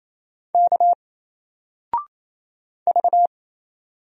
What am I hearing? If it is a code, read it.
Morse code, heard as a keyed tone, about 27 wpm, 700 Hz. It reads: K V
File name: Single-Letter-Number-28wpm